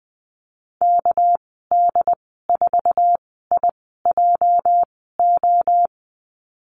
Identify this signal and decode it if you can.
Morse code, heard as a keyed tone, about 20 wpm, spaced as for 15 wpm, 700 Hz. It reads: KD4IJO
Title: 219 Lesson 35g - Number 0 - Call Signs 20wpm